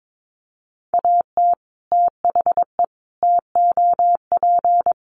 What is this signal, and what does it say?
Morse code, heard as a keyed tone, about 22 wpm, 700 Hz. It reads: AT THE TOP
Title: Sets of 4 Words 22wpm